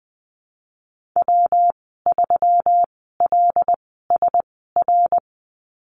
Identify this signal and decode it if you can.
Morse code, heard as a keyed tone, about 20 wpm, spaced as for 15 wpm, 700 Hz. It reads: W3LSR
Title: 154 Lesson 25g - Letter p - Call Signs 20wpm